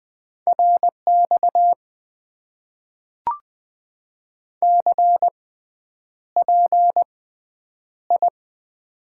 Morse code, heard as a keyed tone, about 20 wpm, spaced as for 9 wpm, 700 Hz. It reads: RX CPI
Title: Top 90 QSO Elements - 6x Character Spacing - 20wpm